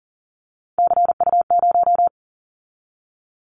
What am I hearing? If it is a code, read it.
Morse code, heard as a keyed tone, about 40 wpm, 690 Hz. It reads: CU0
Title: Callsign Prefixes for Top 100 DXCC - 40wpm